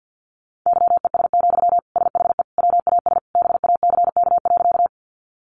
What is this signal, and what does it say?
Morse code, heard as a keyed tone, about 50 wpm, 700 Hz. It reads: YES, SHE WAS BACK.
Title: Sentences from Top 500 Words 50wpm